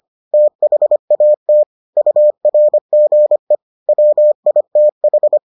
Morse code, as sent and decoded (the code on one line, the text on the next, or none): - .... .- - / ..- .-. --. / .-- .. - ....
THAT URGE WITH